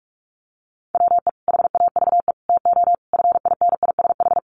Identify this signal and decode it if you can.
Morse code, heard as a keyed tone, about 44 wpm, 700 Hz. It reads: WE HAVE TO FINISH
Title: Sentences from Top 1200 Words 45wpm